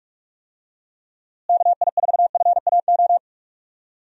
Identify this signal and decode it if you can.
Morse code, heard as a keyed tone, about 44 wpm, 700 Hz. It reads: KI4FAO
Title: USA Call Signs - 6 Characters - Repeated 3x - 45wpm